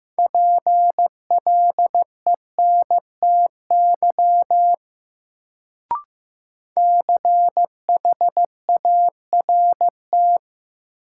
Morse code, heard as a keyed tone, about 16 wpm, 700 Hz. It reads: PLENTY CHART
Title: Sentences from Top 1300 Words - New Word Drill - 15wpm